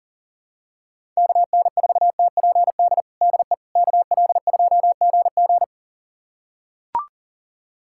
Morse code, heard as a keyed tone, about 40 wpm, 700 Hz. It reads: KN4TPD DE KL2GG
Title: USA Callsign DE Callsign 40wpm